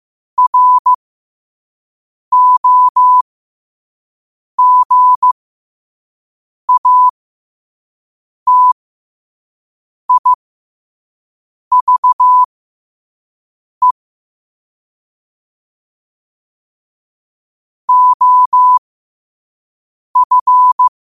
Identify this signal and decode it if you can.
Morse code, heard as a keyed tone, about 15 wpm, spaced as for 5 wpm, 1000 Hz. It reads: ROGATIVE OF
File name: qotdfiles / 5wpm